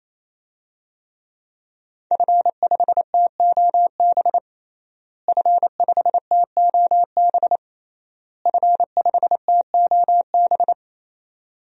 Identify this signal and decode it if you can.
Morse code, heard as a keyed tone, about 28 wpm, 700 Hz. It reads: F5TOB F5TOB F5TOB
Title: DXCC-100 Call Signs - 5 Characters - Repeated 3x - 28wpm